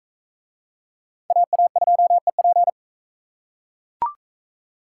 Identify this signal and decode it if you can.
Morse code, heard as a keyed tone, about 38 wpm, 700 Hz. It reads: AA2EP